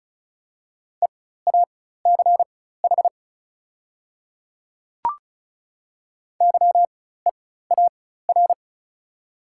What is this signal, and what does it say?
Morse code, heard as a keyed tone, about 36 wpm, spaced as for 16 wpm, 690 Hz. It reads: EACH YEAR
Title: 4x Character Spacing - Top 200 Words - 35wpm